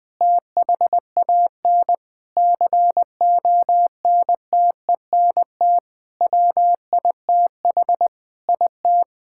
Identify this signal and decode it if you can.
Morse code, heard as a keyed tone, about 21 wpm, 700 Hz. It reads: THAN CONTENT WITH IT
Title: Sentences from Top 1000 Words - Review Part 6 - 20wpm